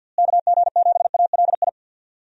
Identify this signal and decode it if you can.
Morse code, heard as a keyed tone, about 50 wpm, 700 Hz. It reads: KG7ALI
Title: USA Call Signs - 6 Characters - Repeated 2x - 50wpm